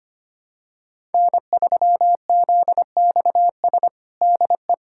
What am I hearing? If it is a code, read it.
Morse code, heard as a keyed tone, about 25 wpm, 700 Hz. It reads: N3ZXS DE
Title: USA Callsign DE Callsign - 5 Characters - 25wpm